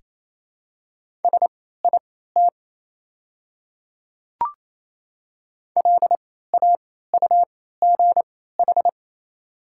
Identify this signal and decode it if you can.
Morse code, heard as a keyed tone, about 28 wpm, spaced as for 16 wpm, 700 Hz. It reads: SIT LAUGH
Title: 3x Character Spacing - Top 500 Words - 28wpm